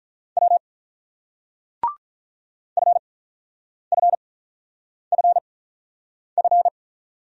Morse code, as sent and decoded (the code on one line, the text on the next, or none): .-- / ..-. ..-. ..-. ..-.
W FFFF